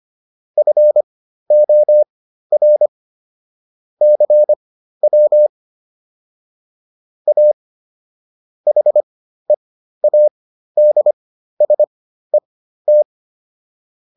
Morse code, heard as a keyed tone, about 25 wpm, spaced as for 13 wpm, 600 Hz. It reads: FOR CW A HEADSET